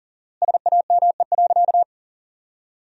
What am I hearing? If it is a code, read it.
Morse code, heard as a keyed tone, about 40 wpm, 700 Hz. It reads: SAME.
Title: 4x Word-Spacing - Sentences from Top 300 Words - 40wpm